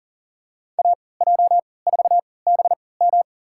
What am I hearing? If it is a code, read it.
Morse code, heard as a keyed tone, about 40 wpm, spaced as for 23 wpm, 700 Hz. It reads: AJ4BM
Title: USA Call Signs - 5 Characters - 3x Character Spacing - 40wpm